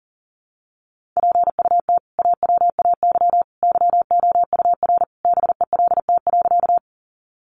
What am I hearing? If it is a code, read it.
Morse code, heard as a keyed tone, about 40 wpm, 700 Hz. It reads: PUT AWAY YOUR BELT.
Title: Sentences from Top 1400 Words 40wpm